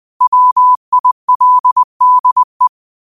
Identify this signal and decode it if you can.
Morse code, heard as a keyed tone, about 20 wpm, 1000 Hz. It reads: WILDE
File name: qotdfiles / 20wpm